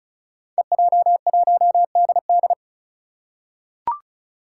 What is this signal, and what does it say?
Morse code, heard as a keyed tone, about 36 wpm, 700 Hz. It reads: EJ1DD
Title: Five Challenging Letter-Number Combinations 35wpm